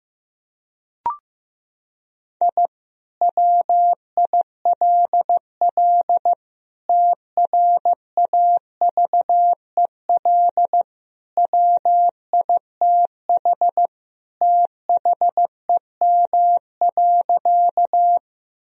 Morse code, heard as a keyed tone, about 15 wpm, 700 Hz. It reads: I WILL TRAVEL WITH THEM.
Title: Sentences from Top 1000 Words - Review Part 2 - 15wpm